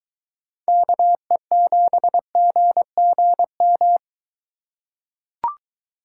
Morse code, heard as a keyed tone, about 22 wpm, 700 Hz. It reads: KE7GGM